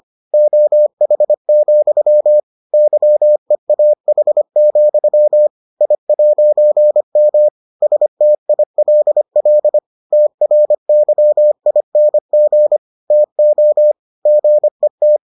Morse code, heard as a keyed tone, about 25 wpm, 600 Hz. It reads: OH, YEAH, I'M STILL TRYING TO GET